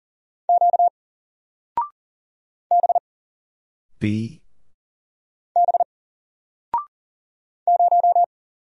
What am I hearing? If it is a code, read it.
Morse code, heard as a keyed tone, about 38 wpm, 700 Hz. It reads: Q B B 0
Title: Single Letter-Number 40wpm